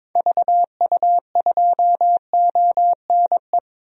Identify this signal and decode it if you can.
Morse code, heard as a keyed tone, about 22 wpm, 700 Hz. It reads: VU2ONE
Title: DXCC-100 Call Signs - 6 Characters - 22wpm